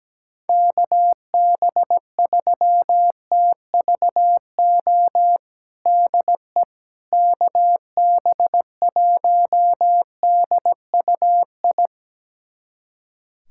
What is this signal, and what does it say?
Morse code, heard as a keyed tone, about 17 wpm, 700 Hz. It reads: KB3TVO DE KB1DUI
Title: USA Callsign DE Callsign - 6 Characters - 17wpm